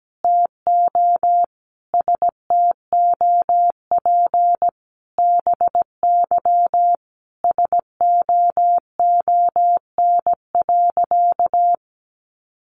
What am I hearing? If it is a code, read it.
Morse code, heard as a keyed tone, about 17 wpm, 700 Hz. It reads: TO STOP BY SOON.
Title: Sentences from Top 1500 Words 17wpm